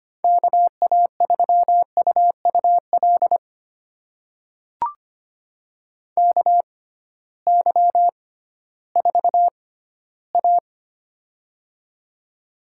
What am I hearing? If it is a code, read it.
Morse code, heard as a keyed tone, about 25 wpm, spaced as for 15 wpm, 700 Hz. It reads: KA3UUL KY4A
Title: USA Call Signs - 6x Character Spacing - 25wpm